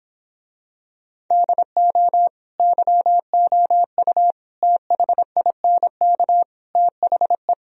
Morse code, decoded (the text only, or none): DO YOU THINK THE